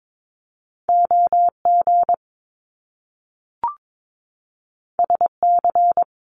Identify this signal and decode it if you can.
Morse code, heard as a keyed tone, about 22 wpm, 700 Hz. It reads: OG SC
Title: Top Two Letter Combinations in Words 22wpm